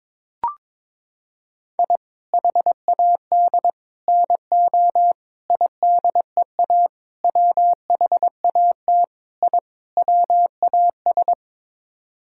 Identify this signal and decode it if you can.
Morse code, heard as a keyed tone, about 21 wpm, 700 Hz. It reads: I HAD NO IDEA WHAT I WAS
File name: practice / 22wpm / Sets of 7 Words - 22wpm